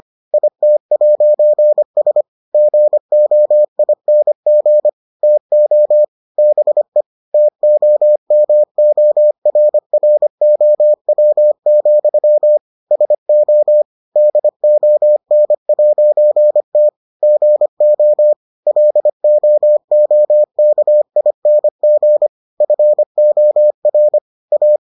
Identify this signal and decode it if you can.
Morse code, heard as a keyed tone, about 25 wpm, 600 Hz. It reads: IT'S GOING TO BE TOMORROW, SO DON'T GO LOOKING FOR A